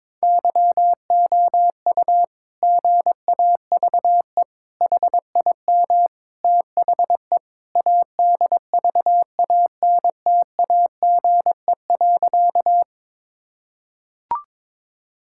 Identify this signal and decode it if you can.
Morse code, heard as a keyed tone, about 21 wpm, 700 Hz. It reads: YOU GAVE HIM THE ADVANTAGE.
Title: Sentences from Top 1000 Words - Review Part 6 - 22wpm